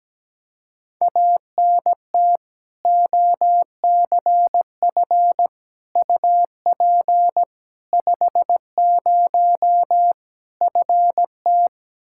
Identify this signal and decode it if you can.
Morse code, heard as a keyed tone, about 17 wpm, 700 Hz. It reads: ANT OCF UP 50 FT
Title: Practice QSO - Exchange 2 - Info - 17wpm